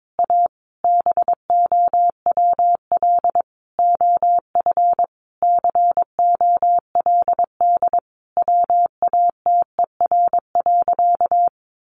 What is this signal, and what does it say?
Morse code, heard as a keyed tone, about 22 wpm, 700 Hz. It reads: A BOWL OF COLD WATER.
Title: Sentences from Top 1200 Words 22wpm